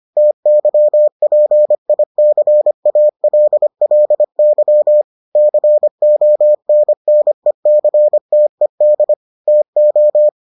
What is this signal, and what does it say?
Morse code, heard as a keyed tone, about 25 wpm, 600 Hz. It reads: TYPICALLY CONNECTED TO